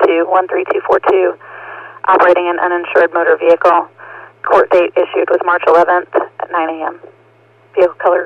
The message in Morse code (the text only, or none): none